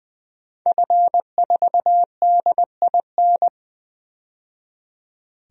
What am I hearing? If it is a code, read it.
Morse code, heard as a keyed tone, about 20 wpm, 700 Hz. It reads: F4DIN
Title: DXCC-100 Call Signs - 5 Characters - 20wpm